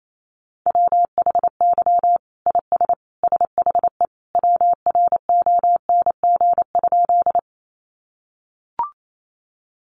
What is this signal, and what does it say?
Morse code, heard as a keyed tone, about 28 wpm, 700 Hz. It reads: WHY IS SHE WRONG?
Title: Sentences from Top 900 Words 28wpm